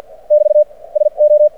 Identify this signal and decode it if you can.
Morse code, heard as a keyed tone, about 48 wpm, spaced as for 39 wpm, 600 Hz. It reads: <BT> SO
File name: archived / CRSnet